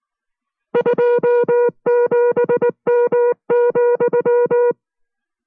Morse code, heard as a keyed tone, about 19 wpm, 480 Hz. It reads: 27M,